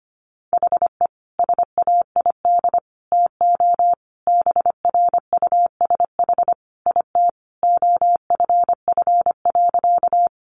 Morse code, heard as a keyed tone, about 25 wpm, 700 Hz. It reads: HE SAID TO BRUSH IT OFF.
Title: Sentences from Top 1100 Words 25wpm